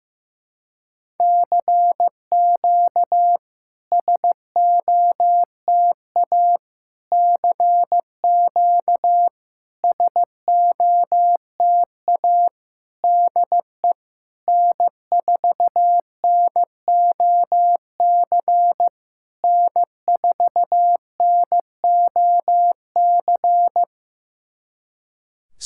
Morse code, heard as a keyed tone, about 15 wpm, 700 Hz. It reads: CQ SOTA CQ SOTA DE N4NOC N4NOC
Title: SOTA QSO Practice 15wpm